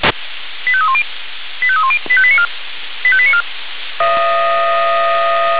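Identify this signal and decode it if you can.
Five-tone selective call (ZVEI), heard as quick runs of five tones, tones of about 70 ms. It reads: 85310 85310 85894 85894
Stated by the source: Received on SDR, ZVEI five-tone code 85320.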